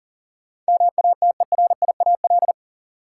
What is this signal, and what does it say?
Morse code, heard as a keyed tone, about 37 wpm, 700 Hz. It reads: MATERIAL
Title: Sentences from Top 1000 Words - Review Part 5 - New Word Drill - 40wpm